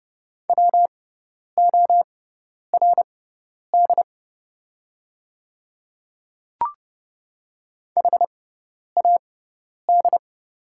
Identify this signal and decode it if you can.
Morse code, heard as a keyed tone, about 30 wpm, spaced as for 10 wpm, 690 Hz. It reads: WORD HAD
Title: 6x Character Spacing - Top 100 Words - 30wpm